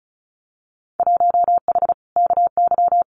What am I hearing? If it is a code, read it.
Morse code, heard as a keyed tone, about 34 wpm, 700 Hz. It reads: 1H KY
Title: Winter Field Day Exchanges 35wpm